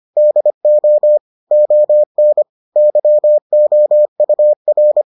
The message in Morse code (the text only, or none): DO ON YOUR